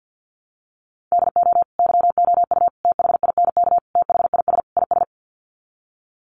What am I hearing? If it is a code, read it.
Morse code, heard as a keyed tone, about 50 wpm, 700 Hz. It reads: DO YOU THINK THIS IS
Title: Sets of 5 Words Encore2 50wpm